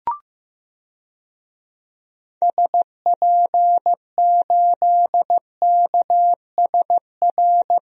Morse code, heard as a keyed tone, about 15 wpm, 700 Hz. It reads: SP8KSR
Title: DXCC-100 Call Signs - 6 Characters - Repeated 3x - 15wpm